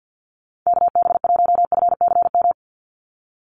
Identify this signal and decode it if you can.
Morse code, heard as a keyed tone, about 50 wpm, 700 Hz. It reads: KB1FCM
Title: USA Call Signs 50wpm